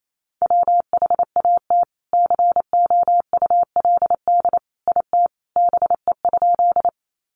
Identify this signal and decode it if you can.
Morse code, heard as a keyed tone, about 28 wpm, 700 Hz. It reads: WHAT COULD IT BE?
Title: Sentences from Top 100 Words 28wpm